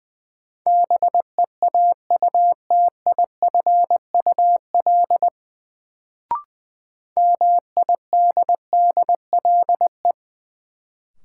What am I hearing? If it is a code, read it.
Morse code, heard as a keyed tone, about 20 wpm, 700 Hz. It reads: BEAUTIFUL MIDDLE
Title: Sentences from Top 900 Words - New Word Drill - 20wpm